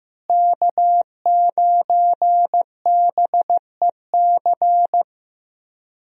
Morse code, heard as a keyed tone, about 15 wpm, 700 Hz. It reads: K9BEC
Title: USA Call Signs - 5 Characters - Repeated 3x - 15wpm